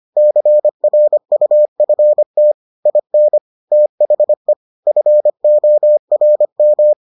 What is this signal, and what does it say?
Morse code, heard as a keyed tone, about 25 wpm, 600 Hz. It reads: CRUFT IN THE FORM